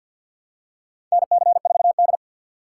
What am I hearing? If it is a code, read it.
Morse code, heard as a keyed tone, about 46 wpm, 700 Hz. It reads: NC4D
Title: USA Call Signs - 4 Characters - 2x Character Spacing - 50wpm